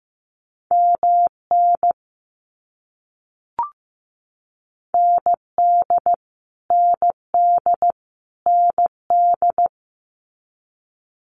Morse code, heard as a keyed tone, about 15 wpm, 700 Hz. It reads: MN ND ND ND
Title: US States Repeated 3x 15wpm